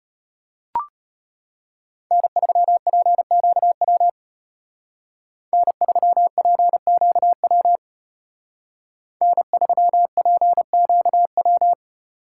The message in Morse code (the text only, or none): N3PQW N3PQW N3PQW